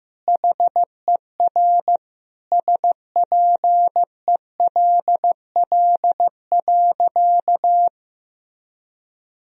Morse code, heard as a keyed tone, about 16 wpm, 700 Hz. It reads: HER SPELL.
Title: Sentences from Top 1000 Words - Review Part 1 - 15wpm